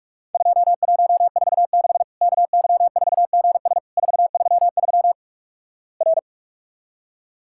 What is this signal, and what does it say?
Morse code, heard as a keyed tone, about 46 wpm, 700 Hz. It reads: 2146 KY4GS 433 R
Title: Worked All Europe DX Contest - QTC Exchange - 45wpm